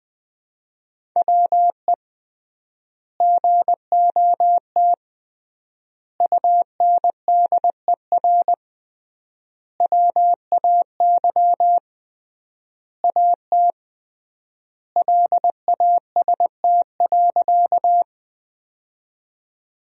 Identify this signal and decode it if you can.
Morse code, heard as a keyed tone, about 21 wpm, 700 Hz. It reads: WE GOT UNDER WAY AT LAST.
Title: 3x Word-Spacing - Sentences from Top 300 Words - 20wpm